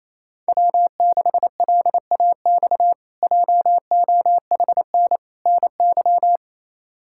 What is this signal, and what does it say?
Morse code, heard as a keyed tone, about 28 wpm, 700 Hz. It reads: W6LAX JOHN NY
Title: K1USN SST QSO Practice - 28wpm